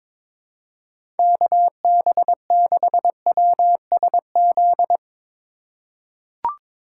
Morse code, heard as a keyed tone, about 21 wpm, 700 Hz. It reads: KB6WSZ
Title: USA Call Signs - 6 Characters - 3x Character Spacing - 22wpm